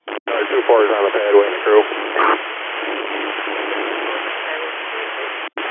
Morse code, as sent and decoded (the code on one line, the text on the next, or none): none